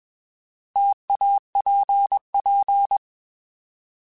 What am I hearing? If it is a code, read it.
Morse code, heard as a keyed tone, about 21 wpm, 800 Hz. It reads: TAPP